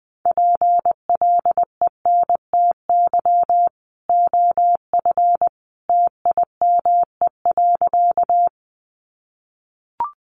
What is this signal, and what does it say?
Morse code, heard as a keyed tone, about 20 wpm, 700 Hz. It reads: PLENTY OF TIME.
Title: Sentences from Top 1300 Words 20wpm